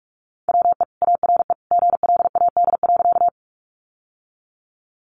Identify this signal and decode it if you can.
Morse code, heard as a keyed tone, about 44 wpm, 700 Hz. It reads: WE ARE GLAD.
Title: Sentences from Top 800 Words 45wpm